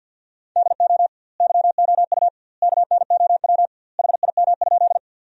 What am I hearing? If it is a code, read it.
Morse code, heard as a keyed tone, about 50 wpm, 700 Hz. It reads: DO YOU KNOW HIM?
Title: 2x Word-Spacing - Sentences from Top 100 Words - 50wpm